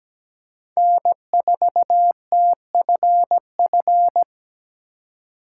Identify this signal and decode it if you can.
Morse code, heard as a keyed tone, about 17 wpm, 700 Hz. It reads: N4TFF
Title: USA Call Signs - 4x Character Spacing - 17wpm